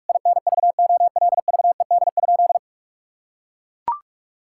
Morse code, heard as a keyed tone, about 42 wpm, 700 Hz. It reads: INVOLVED?
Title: Sentences from Top 1000 Words - Review Part 5 - 45wpm